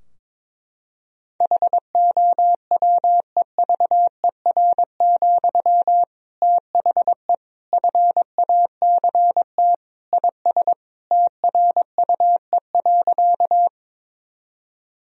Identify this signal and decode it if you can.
Morse code, heard as a keyed tone, about 22 wpm, 700 Hz. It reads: HOWEVER, THE FACT IS TRUE.
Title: Sentences from Top 1000 Words - Review Part 3 - 22wpm